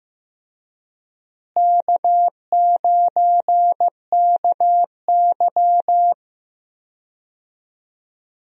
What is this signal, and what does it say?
Morse code, heard as a keyed tone, about 16 wpm, 700 Hz. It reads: K9KY
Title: USA Call Signs - 4 Characters - Repeated 2x - 15wpm